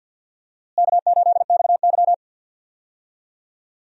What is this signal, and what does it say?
Morse code, heard as a keyed tone, about 50 wpm, 700 Hz. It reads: K8XY
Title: USA Call Signs - 4 Characters - Repeated 3x - 50wpm